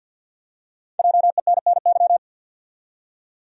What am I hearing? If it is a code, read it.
Morse code, heard as a keyed tone, about 50 wpm, 700 Hz. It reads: JENNY
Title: USA Common Names - 50wpm